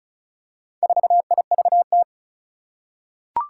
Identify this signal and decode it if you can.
Morse code, heard as a keyed tone, about 36 wpm, 700 Hz. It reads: 4IVT
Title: Four Challenging Letter-Number Combinations 35wpm